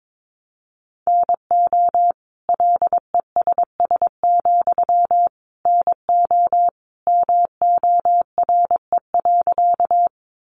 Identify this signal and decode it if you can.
Morse code, heard as a keyed tone, about 22 wpm, 700 Hz. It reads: NO LESS, NO MORE.
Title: Sentences from Top 400 Words 22wpm